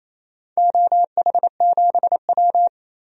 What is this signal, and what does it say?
Morse code, heard as a keyed tone, about 28 wpm, 700 Hz. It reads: OH7W
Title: DXCC-100 Call Signs - 4 Characters - Repeated 2x - 28wpm